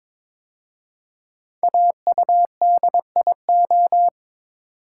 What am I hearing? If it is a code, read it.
Morse code, heard as a keyed tone, about 22 wpm, 700 Hz. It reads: AUDIO
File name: 5 Letter Words - First 3 Letters Given - 22wpm